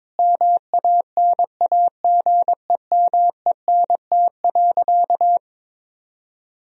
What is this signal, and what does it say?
Morse code, heard as a keyed tone, about 21 wpm, 700 Hz. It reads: MANAGEMENT.
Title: Sentences from Top 1000 Words - Review Part 4 - 22wpm